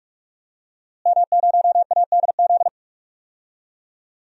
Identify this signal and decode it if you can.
Morse code, heard as a keyed tone, about 43 wpm, 700 Hz. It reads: M0ADZ
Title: DXCC-100 Call Signs - 5 Characters - Repeated 3x - 45wpm